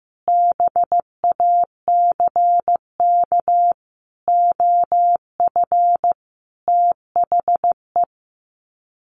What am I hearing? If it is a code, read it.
Morse code, heard as a keyed tone, about 14 wpm, 700 Hz. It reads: BACK OF THE
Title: Sets of 5 Words Encore 15wpm